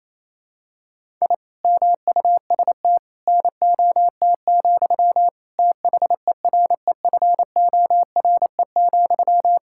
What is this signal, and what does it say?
Morse code, heard as a keyed tone, about 28 wpm, 700 Hz. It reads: I MUST NOT, THEREFORE,
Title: Sentences from Top 1000 Words - Review Part 5 - 28wpm